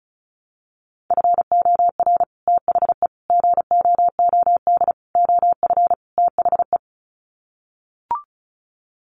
Morse code, heard as a keyed tone, about 34 wpm, 700 Hz. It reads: FOR THE GOOD OF THE
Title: Sets of 5 Words Encore2 35wpm